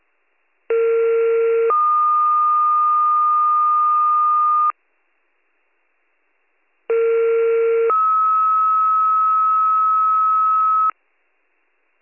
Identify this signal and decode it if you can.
Two-tone page, heard as a one-second tone, then a three-second tone, about 460 then 1220 Hz, about 460 then 1290 Hz. It is heard from 0.7 to 10.9 s.